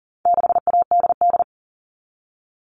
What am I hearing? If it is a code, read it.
Morse code, heard as a keyed tone, about 40 wpm, 700 Hz. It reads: 6ADD
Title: All Cut Numbers - 4-Digit 40wpm